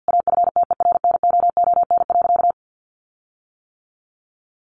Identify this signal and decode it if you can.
Morse code, heard as a keyed tone, about 50 wpm, 700 Hz. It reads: AFTERNOON.
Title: Sentences from Top 1500 Words 50wpm